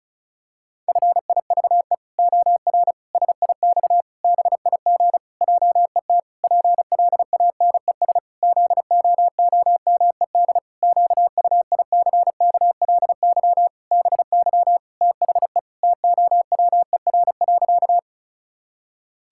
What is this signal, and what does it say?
Morse code, heard as a keyed tone, about 36 wpm, 700 Hz. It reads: FIVE OR SIX BIG JET PLANES ZOOMED QUICKLY BY THE TOWER.
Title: Pangram Challenge v2 - 35wpm